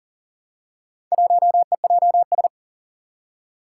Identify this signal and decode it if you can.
Morse code, heard as a keyed tone, about 40 wpm, 700 Hz. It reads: 1EJS
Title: Four Challenging Letter-Number Combinations 40wpm